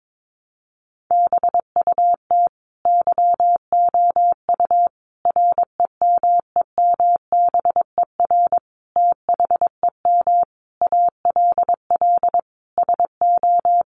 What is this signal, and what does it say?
Morse code, heard as a keyed tone, about 22 wpm, 700 Hz. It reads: BUT YOU REMEMBER THEM ALL SO
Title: Sentences from Top 1300 Words 22wpm